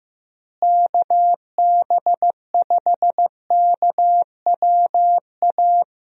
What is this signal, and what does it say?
Morse code, heard as a keyed tone, about 15 wpm, 700 Hz. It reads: KB5KWA
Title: USA Call Signs - 6 Characters - Repeated 3x - 15wpm